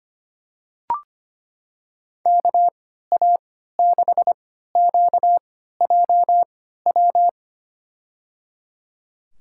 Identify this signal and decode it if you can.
Morse code, heard as a keyed tone, about 25 wpm, spaced as for 14 wpm, 700 Hz. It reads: KA6QJW
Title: USA Call Signs - 6 Characters - 3x Character Spacing - 25wpm